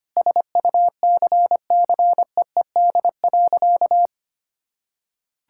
Morse code, heard as a keyed tone, about 25 wpm, 700 Hz. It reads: SUCCEED.
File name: practice / 25wpm / Sentences with 2-Word Phrases - 25wpm